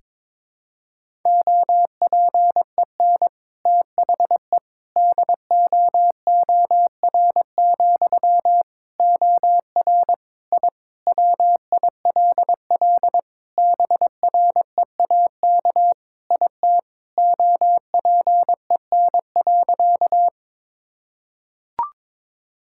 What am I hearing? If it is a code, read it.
Morse code, heard as a keyed tone, about 22 wpm, 700 Hz. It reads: OPEN THE DOOR, OR I WILL BREAK IT OPEN.
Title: Sentences from Top 1000 Words - Review Part 6 - 22wpm